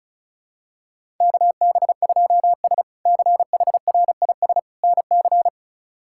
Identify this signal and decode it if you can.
Morse code, heard as a keyed tone, about 36 wpm, 700 Hz. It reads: KB2S CHRIS NC
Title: K1USN SST QSO Practice - 35wpm